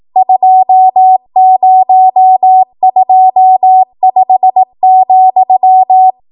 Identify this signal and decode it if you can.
Morse code, heard as a keyed tone, about 18 wpm, 750 Hz. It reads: 2025,
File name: Morse / 18WPM